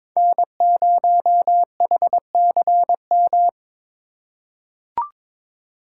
Morse code, heard as a keyed tone, about 22 wpm, 700 Hz. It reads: N0HCM
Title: USA Call Signs - 5 Characters - 3x Character Spacing - 22wpm